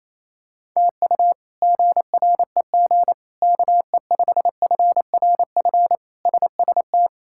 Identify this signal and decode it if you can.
Morse code, heard as a keyed tone, about 28 wpm, 700 Hz. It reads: TU GREG KE5FRF SST
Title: K1USN SST QSO Practice - 28wpm